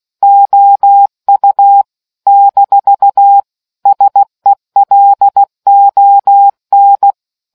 Morse code, heard as a keyed tone, about 16 wpm, 800 Hz. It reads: OU - SELON